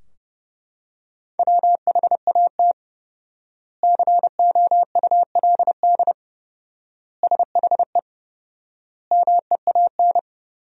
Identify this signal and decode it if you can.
Morse code, heard as a keyed tone, about 32 wpm, 700 Hz. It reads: WHAT COULD SHE MEAN
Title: 4x Word-Spacing - Sentences from Top 200 Words - 30wpm